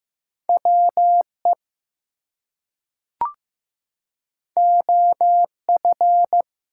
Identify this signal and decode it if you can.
Morse code, heard as a keyed tone, about 15 wpm, 700 Hz. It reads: WE OF
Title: Two Letter Words - 15wpm